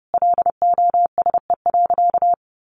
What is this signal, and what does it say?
Morse code, heard as a keyed tone, about 30 wpm, 700 Hz. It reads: LOSE.
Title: Sentences from Top 1100 Words 30wpm